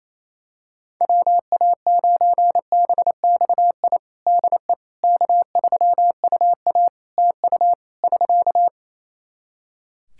Morse code, heard as a keyed tone, about 28 wpm, 700 Hz. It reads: WA9BXI DE K3UA TU <SK>
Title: K1USN SST QSO Practice 28wpm